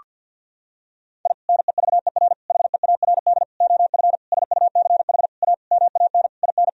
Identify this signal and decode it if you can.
Morse code, heard as a keyed tone, about 48 wpm, 700 Hz. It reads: I NEVER HEARD OF SUCH A MAN IN
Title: Sentences from Top 1000 Words - Review Part 3 - 50wpm